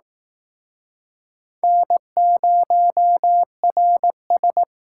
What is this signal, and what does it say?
Morse code, heard as a keyed tone, about 19 wpm, 700 Hz. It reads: N0RS